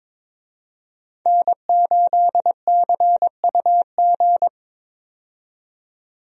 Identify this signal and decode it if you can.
Morse code, heard as a keyed tone, about 22 wpm, 700 Hz. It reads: N8CUG
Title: USA Call Signs - 5 Characters - 22wpm